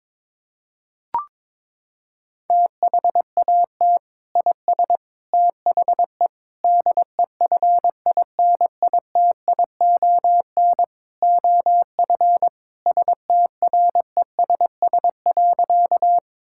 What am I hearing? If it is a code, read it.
Morse code, heard as a keyed tone, about 22 wpm, 700 Hz. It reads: THAT IS THE DEFINITION OF STRESS.
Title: Sentences from Top 1000 Words - Review Part 6 - 22wpm